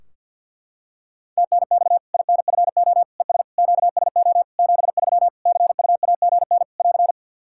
Morse code, heard as a keyed tone, about 53 wpm, 700 Hz. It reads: TNX INFO ES QSO 73 CUAGN <AR>
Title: Practice QSO - Exchange 4 - Ending - 50wpm